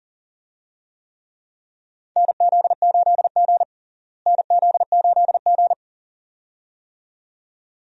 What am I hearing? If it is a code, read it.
Morse code, heard as a keyed tone, about 40 wpm, 700 Hz. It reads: NZ8G NZ8G
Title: USA Call Signs - 4 Characters - Repeated 2x - 40wpm